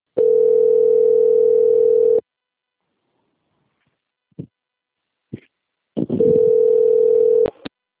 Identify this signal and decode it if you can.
Ringback tone (North America), 440 + 480 Hz.